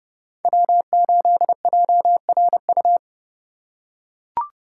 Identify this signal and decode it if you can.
Morse code, heard as a keyed tone, about 30 wpm, 700 Hz. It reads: W8JRU